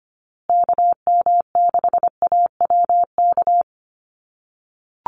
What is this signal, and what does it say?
Morse code, heard as a keyed tone, about 25 wpm, 700 Hz. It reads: KM6AWK
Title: USA Call Signs Repeated 2x 25wpm